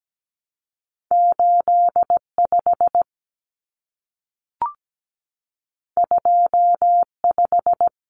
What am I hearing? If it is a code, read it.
Morse code, heard as a keyed tone, about 17 wpm, 700 Hz. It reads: 85 25
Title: Common Cut Numbers - 2-Digit 17wpm